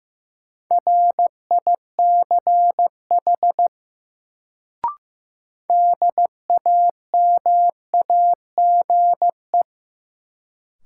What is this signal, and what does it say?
Morse code, heard as a keyed tone, about 16 wpm, 700 Hz. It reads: RICH DAMAGE
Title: Sentences from Top 1100 Words - New Word Drill - 15wpm